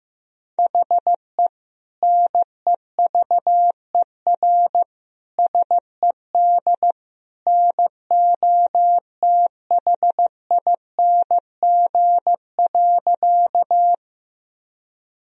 Morse code, heard as a keyed tone, about 15 wpm, 700 Hz. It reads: HE NEVER SED NOTHING.
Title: Sentences from Top 500 Words and Abbreviations 15wpm